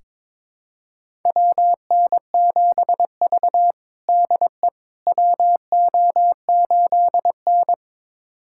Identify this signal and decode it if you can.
Morse code, heard as a keyed tone, about 22 wpm, 700 Hz. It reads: WN7V DE WO8N
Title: USA Callsign DE Callsign - 4 Characters - 22wpm